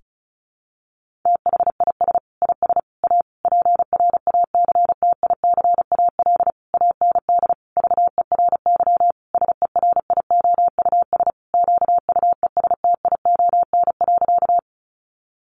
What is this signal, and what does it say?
Morse code, heard as a keyed tone, about 34 wpm, 700 Hz. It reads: THIS IS A PRACTICAL AND VERY SERIOUS QUESTION.
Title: Sentences from Top 1400 Words 35wpm